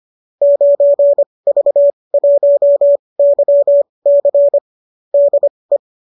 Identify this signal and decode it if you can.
Morse code, heard as a keyed tone, about 25 wpm, spaced as for 20 wpm, 590 Hz. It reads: 9V1YC DE